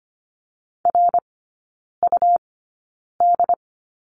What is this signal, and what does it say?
Morse code, heard as a keyed tone, about 25 wpm, spaced as for 9 wpm, 700 Hz. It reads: RUD
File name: Lesson 17e - Letter u - Three Characters 25wpm